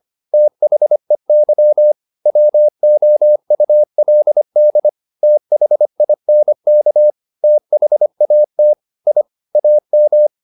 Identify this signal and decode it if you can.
Morse code, heard as a keyed tone, about 25 wpm, 600 Hz. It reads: THEY WOULD THINK THAT I AM
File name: the old man and the sea part 1 25wpm